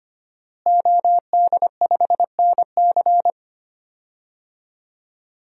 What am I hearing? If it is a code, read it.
Morse code, heard as a keyed tone, about 25 wpm, 700 Hz. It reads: OD5NC
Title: DXCC-100 Call Signs - 5 Characters - Repeated 2x - 25wpm